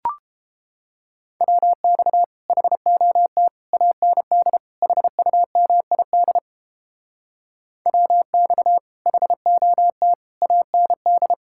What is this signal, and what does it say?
Morse code, heard as a keyed tone, about 29 wpm, 700 Hz. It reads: WX HOT AND HUMID WX HOT AND